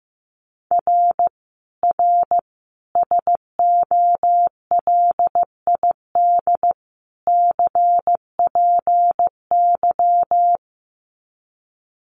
Morse code, heard as a keyed tone, about 15 wpm, 700 Hz. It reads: R R SOLID CPY
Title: Common QSO Phrases 15wpm